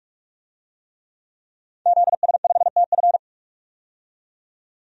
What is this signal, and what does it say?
Morse code, heard as a keyed tone, about 45 wpm, 700 Hz. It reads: ZS5TF
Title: DXCC-100 Call Signs - 5 Characters - 45wpm